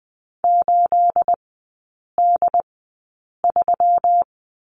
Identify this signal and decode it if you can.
Morse code, heard as a keyed tone, about 20 wpm, spaced as for 8 wpm, 700 Hz. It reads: 8D3
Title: 218 Lesson 35e - Number 0 - Three Characters 20wpm